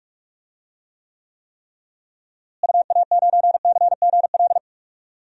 Morse code, heard as a keyed tone, about 46 wpm, 700 Hz. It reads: UA9CGL
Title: DXCC-100 Call Signs - 6 Characters - Repeated 3x - 45wpm